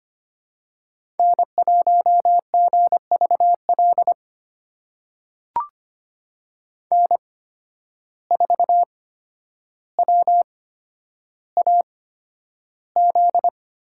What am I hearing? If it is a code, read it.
Morse code, heard as a keyed tone, about 25 wpm, spaced as for 11 wpm, 700 Hz. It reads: N1GVL N4WAZ
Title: USA Call Signs - 5 Characters - 8x Character Spacing - 25wpm